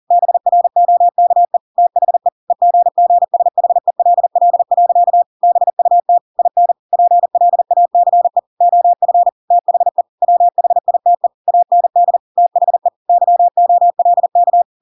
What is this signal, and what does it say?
Morse code, heard as a keyed tone, about 40 wpm, 700 Hz. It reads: BROKE THE EGGSHELL. BUT IN PLACE OF THE WHITE AND THE YOLK
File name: Pinocchio / Pinocchio ch5 40 40 ik1hge